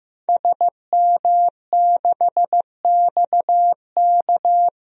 Morse code, heard as a keyed tone, about 15 wpm, 700 Hz. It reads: SM6XK
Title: DXCC-100 Call Signs - 5 Characters - 15wpm